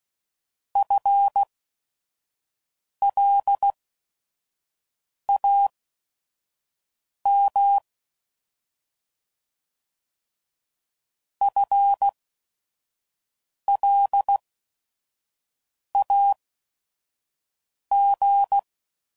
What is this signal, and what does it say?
Morse code, heard as a keyed tone, about 16 wpm, spaced as for 5 wpm, 800 Hz. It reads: FLAM FLAG
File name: Prefixes / By First Letter / Fs